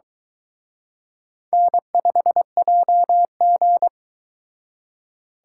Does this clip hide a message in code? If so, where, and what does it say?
Yes: Morse code from 1.5 to 3.9 s, heard as a keyed tone, about 23 wpm, 700 Hz. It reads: N5JG